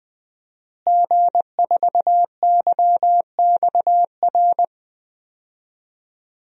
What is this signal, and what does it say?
Morse code, heard as a keyed tone, about 20 wpm, 700 Hz. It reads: G4YXR
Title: DXCC-100 Call Signs - 5 Characters - Repeated 2x - 20wpm